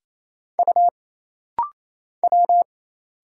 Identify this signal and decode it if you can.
Morse code, heard as a keyed tone, about 28 wpm, 700 Hz. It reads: U W
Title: Single Letter-Number Alt - Rapid-Fire 28wpm